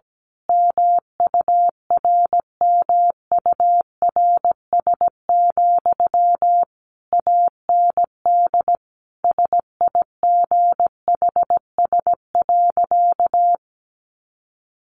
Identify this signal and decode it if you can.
Morse code, heard as a keyed tone, about 17 wpm, 700 Hz. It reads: MURMURS, AND SIGHS.
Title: A Sound Of Thunder 17wpm